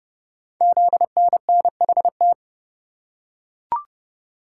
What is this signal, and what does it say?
Morse code, heard as a keyed tone, about 28 wpm, 700 Hz. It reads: ZNNHT
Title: Code Groups - 5 Letters - 30wpm